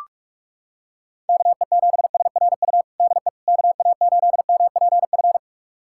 Morse code, heard as a keyed tone, about 42 wpm, 700 Hz. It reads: KE7SRU DE KA8MPF
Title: USA Callsign DE Callsign 45wpm